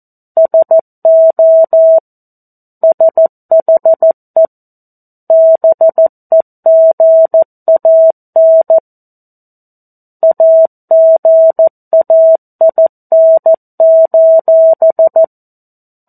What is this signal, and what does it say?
Morse code, heard as a keyed tone, about 14 wpm, 650 Hz. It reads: SO SHE BEGAN AGAIN:
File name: 02-Chapter-02 14 WPM